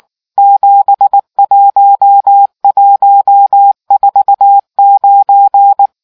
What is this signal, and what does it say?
Morse code, heard as a keyed tone, about 19 wpm, 800 Hz. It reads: 71149